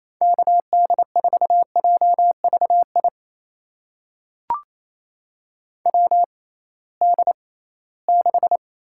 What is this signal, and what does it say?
Morse code, heard as a keyed tone, about 28 wpm, spaced as for 18 wpm, 700 Hz. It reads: KD4JVI WD6
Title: USA Call Signs - 6 Characters - 6x Character Spacing - 28wpm